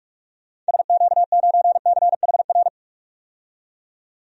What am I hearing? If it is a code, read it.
Morse code, heard as a keyed tone, about 41 wpm, 700 Hz. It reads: SQ9CHR
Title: DXCC-100 Call Signs - 6 Characters - 45wpm